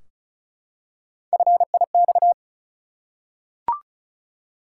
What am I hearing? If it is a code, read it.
Morse code, heard as a keyed tone, about 34 wpm, 700 Hz. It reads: FIX
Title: Three Letter Words - 35wpm